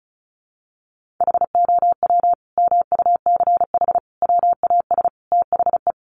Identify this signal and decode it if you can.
Morse code, heard as a keyed tone, about 34 wpm, 700 Hz. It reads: HOW MUCH WAS THE
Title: Sentences from Top 1400 Words 35wpm